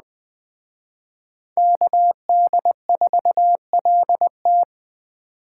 Morse code, heard as a keyed tone, about 20 wpm, 700 Hz. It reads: KD4LT
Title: Speed Racing - USA 2x2 Call Signs 17wpm